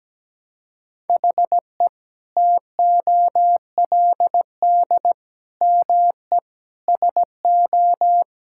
Morse code, heard as a keyed tone, about 17 wpm, 700 Hz. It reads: HE TOLD ME SO
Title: Sentences from Top 1600 Words 17wpm